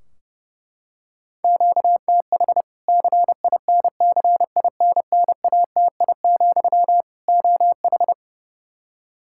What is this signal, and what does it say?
Morse code, heard as a keyed tone, about 30 wpm, 700 Hz. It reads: QTH CINCINNATI, OH
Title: Practice QSO - Exchange 1 - Beginning - 30wpm